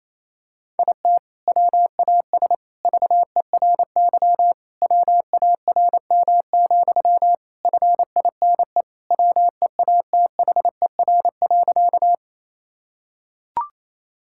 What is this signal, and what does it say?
Morse code, heard as a keyed tone, about 28 wpm, 700 Hz. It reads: IT WAS VERY WARM, FINE WEATHER.
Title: Sentences from Top 1000 Words - Review Part 5 - 28wpm